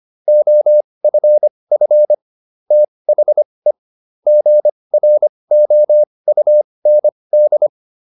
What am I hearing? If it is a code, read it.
Morse code, heard as a keyed tone, about 25 wpm, spaced as for 20 wpm, 600 Hz. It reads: OFF THE GROUND